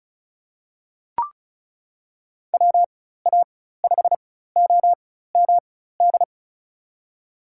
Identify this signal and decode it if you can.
Morse code, heard as a keyed tone, about 36 wpm, spaced as for 16 wpm, 700 Hz. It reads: WA5OMD
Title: USA Call Signs - 4x Character Spacing - 35wpm